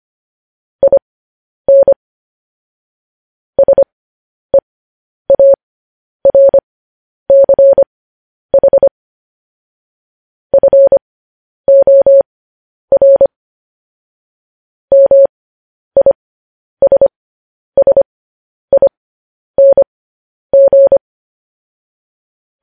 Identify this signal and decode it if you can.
Morse code, heard as a keyed tone, about 25 wpm, spaced as for 10 wpm, 550 Hz. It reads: IN SEARCH FOR MISSING
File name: podcast / News headlines-1x-25x10